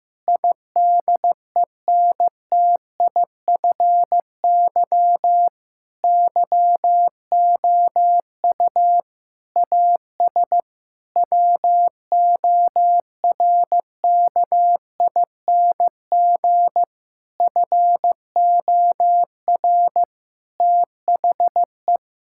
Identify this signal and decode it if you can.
Morse code, heard as a keyed tone, about 15 wpm, 700 Hz. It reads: IDENTIFY YOU AS WORKING FOR THE
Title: Sentences from Top 1000 Words - Review Part 6 - 15wpm